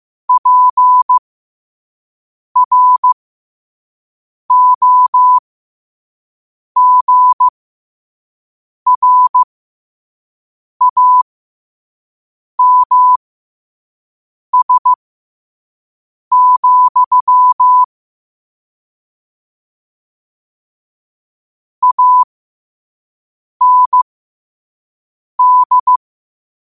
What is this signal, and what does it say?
Morse code, heard as a keyed tone, about 15 wpm, spaced as for 5 wpm, 1000 Hz. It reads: PROGRAMS, AND
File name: qotdfiles / 5wpm